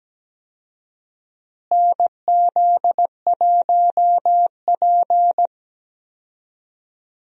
Morse code, heard as a keyed tone, about 17 wpm, 700 Hz. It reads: NZ1P
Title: USA Call Signs - 4 Characters - Repeated 3x - 17wpm